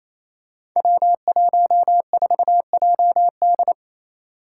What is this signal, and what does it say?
Morse code, heard as a keyed tone, about 28 wpm, 700 Hz. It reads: W14JD
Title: Five Challenging Letter-Number Combinations 28wpm